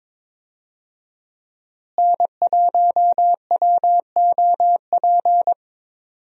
Morse code, heard as a keyed tone, about 22 wpm, 700 Hz. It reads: N1WOP
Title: USA Call Signs - 5 Characters - Repeated 3x - 22wpm